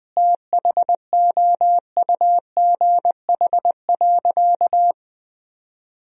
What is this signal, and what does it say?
Morse code, heard as a keyed tone, about 20 wpm, 700 Hz. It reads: THOUGH.
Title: Sentences from Top 1000 Words - Review Part 2 - 20wpm